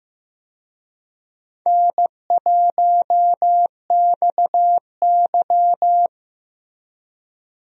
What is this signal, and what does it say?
Morse code, heard as a keyed tone, about 15 wpm, 700 Hz. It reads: N1XY
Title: USA Call Signs - 4 Characters - Repeated 3x - 15wpm